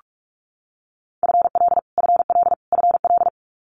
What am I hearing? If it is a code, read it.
Morse code, heard as a keyed tone, about 46 wpm, 700 Hz. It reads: FL FL FL
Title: Multi-letter Phonemes Repeated 3x 45wpm